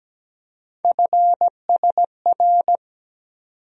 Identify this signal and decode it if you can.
Morse code, heard as a keyed tone, about 17 wpm, 700 Hz. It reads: FSR
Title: USA Call Sign Postfixes - 3 Letters 17wpm